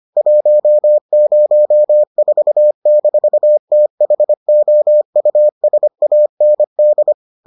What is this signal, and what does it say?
Morse code, heard as a keyed tone, about 25 wpm, 600 Hz. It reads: 104-THOUSAND